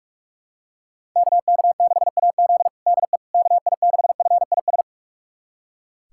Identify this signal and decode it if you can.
Morse code, heard as a keyed tone, about 46 wpm, 690 Hz. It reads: KK6AZ DE KI6FIS
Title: USA Callsign DE Callsign 45wpm